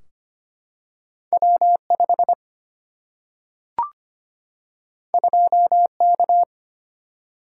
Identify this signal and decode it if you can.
Morse code, heard as a keyed tone, about 25 wpm, 700 Hz. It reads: W 5 2 K
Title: Single-Letter Single-Number 25wpm